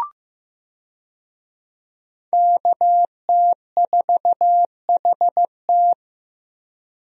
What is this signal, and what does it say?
Morse code, heard as a keyed tone, about 15 wpm, 700 Hz. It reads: KT4HT
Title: USA Call Signs - 5 Characters - Repeated 2x - 15wpm